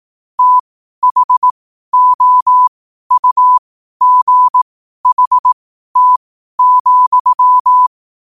Morse code, heard as a keyed tone, about 18 wpm, spaced as for 12 wpm, 1000 Hz. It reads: THOUGHT,
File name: qotdfiles / 12wpm